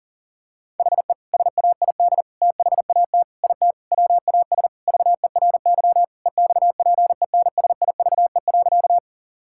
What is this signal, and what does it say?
Morse code, heard as a keyed tone, about 40 wpm, 700 Hz. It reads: HE SAID THAT IT WAS VERY EXPENSIVE.
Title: Sentences from Top 1000 Words - Review Part 6 - 40wpm